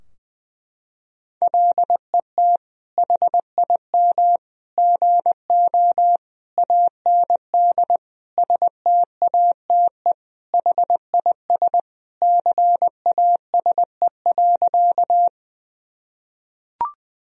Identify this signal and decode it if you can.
Morse code, heard as a keyed tone, about 20 wpm, 700 Hz. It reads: LET HIM GO AND STATE HIS CASE.
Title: Sentences from Top 1000 Words - Review Part 2 - 20wpm